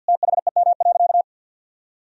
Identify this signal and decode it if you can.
Morse code, heard as a keyed tone, about 52 wpm, 700 Hz. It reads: THEM.
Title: Sentences from Top 300 Words and Abbreviations 50wpm